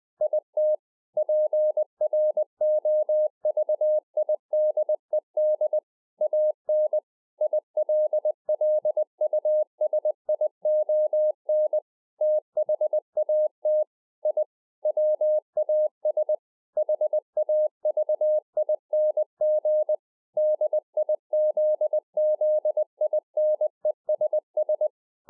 Morse code, heard as a keyed tone, about 20 wpm, 610 Hz. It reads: IT PROVIDED AN ILLUSION THAT I WAS HAVING DIZZINESS